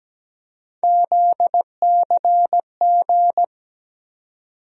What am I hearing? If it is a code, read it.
Morse code, heard as a keyed tone, about 17 wpm, 700 Hz. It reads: ZCG
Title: USA Call Sign Postfixes - 3 Letters 17wpm